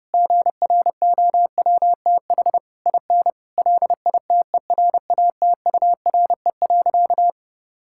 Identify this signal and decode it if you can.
Morse code, heard as a keyed tone, about 32 wpm, 700 Hz. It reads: GROWTH IN LITERATURE.
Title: Sentences from Top 1000 Words - Review Part 5 - 30wpm